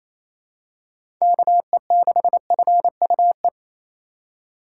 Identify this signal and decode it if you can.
Morse code, heard as a keyed tone, about 28 wpm, 700 Hz. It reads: KE6FUE
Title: USA Call Signs - 6 Characters - 3x Character Spacing - 28wpm